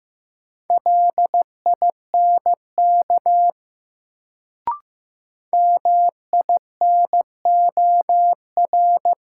Morse code, heard as a keyed tone, about 15 wpm, 700 Hz. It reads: LINK MINOR
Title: Sentences from Top 1500 Words - New Word Drill Alt - 15wpm